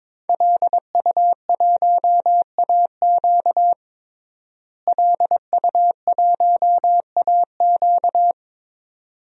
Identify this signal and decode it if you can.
Morse code, heard as a keyed tone, about 22 wpm, 700 Hz. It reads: LU1AQ LU1AQ
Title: DXCC-100 Call Signs - 5 Characters - Repeated 2x - 22wpm